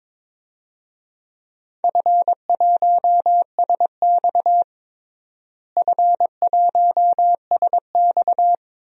Morse code, heard as a keyed tone, about 22 wpm, 700 Hz. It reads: F1SX F1SX
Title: DXCC-100 Call Signs - 4 Characters - Repeated 3x - 22wpm